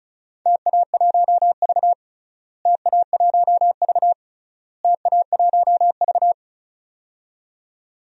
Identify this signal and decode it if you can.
Morse code, heard as a keyed tone, about 34 wpm, 700 Hz. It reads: TA1V TA1V TA1V
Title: DXCC-100 Call Signs - 4 Characters - Repeated 3x - 35wpm